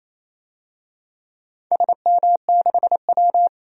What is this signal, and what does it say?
Morse code, heard as a keyed tone, about 28 wpm, 700 Hz. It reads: SM6W
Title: DXCC-100 Call Signs - 4 Characters - Repeated 3x - 28wpm